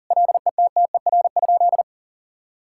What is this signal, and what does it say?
Morse code, heard as a keyed tone, about 40 wpm, 700 Hz. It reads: LETTER?
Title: Sentences from Top 1000 Words - Review Part 6 - 40wpm